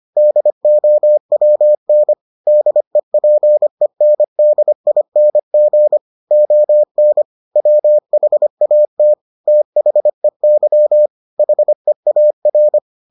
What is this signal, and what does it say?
Morse code, heard as a keyed tone, about 25 wpm, 600 Hz. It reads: DOWN DEPENDING ON WHAT THEY HEAR